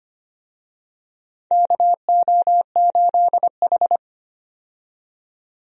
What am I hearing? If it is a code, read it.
Morse code, heard as a keyed tone, about 25 wpm, 700 Hz. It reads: KO8H
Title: USA Call Signs - 4 Characters - 25wpm